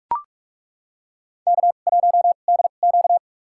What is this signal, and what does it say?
Morse code, heard as a keyed tone, about 42 wpm, spaced as for 31 wpm, 700 Hz. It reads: K1DQ
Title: USA Call Signs - 4 Characters - 2x Character Spacing - 45wpm